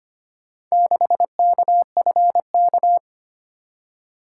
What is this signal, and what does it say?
Morse code, heard as a keyed tone, about 25 wpm, 700 Hz. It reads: 6KFK
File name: practice / 25wpm / Four Letter-Number Combinations - 25wpm